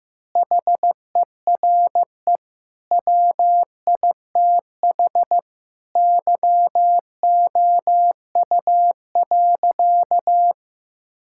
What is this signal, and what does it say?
Morse code, heard as a keyed tone, about 15 wpm, 700 Hz. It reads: HERE WITH YOU.
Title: Sentences from Top 1600 Words 15wpm